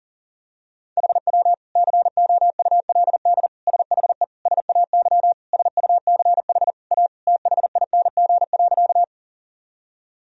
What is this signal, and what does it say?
Morse code, heard as a keyed tone, about 40 wpm, 700 Hz. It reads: HW COULD SHE SAY SUCH A THING.
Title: Sentences from Top 200 Words and Abbreviations 40wpm